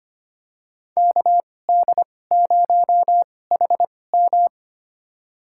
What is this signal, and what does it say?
Morse code, heard as a keyed tone, about 25 wpm, spaced as for 18 wpm, 700 Hz. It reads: KD0HM
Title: USA Call Signs - 5 Characters - 2x Character Spacing - 25wpm